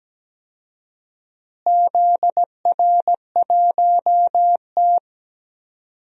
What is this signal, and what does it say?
Morse code, heard as a keyed tone, about 17 wpm, 700 Hz. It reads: ZR1T
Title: DXCC-100 Call Signs - 4 Characters - Repeated 3x - 17wpm